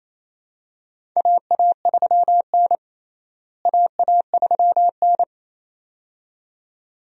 Morse code, heard as a keyed tone, about 27 wpm, 700 Hz. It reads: AA3N AA3N
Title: USA Call Signs - 4 Characters - Repeated 2x - 28wpm